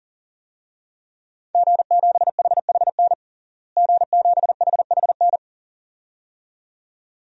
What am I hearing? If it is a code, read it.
Morse code, heard as a keyed tone, about 40 wpm, 700 Hz. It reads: G7HHN G7HHN
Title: DXCC-100 Call Signs - 5 Characters - Repeated 2x - 40wpm